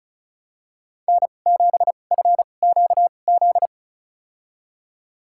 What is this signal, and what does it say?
Morse code, heard as a keyed tone, about 36 wpm, spaced as for 25 wpm, 700 Hz. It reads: N7FQZ
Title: USA Call Signs - 2x Character Spacing - 35wpm